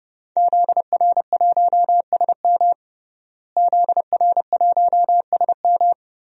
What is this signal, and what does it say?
Morse code, heard as a keyed tone, about 30 wpm, 700 Hz. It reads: ZR1SM ZR1SM
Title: DXCC-100 Call Signs - 5 Characters - Repeated 3x - 30wpm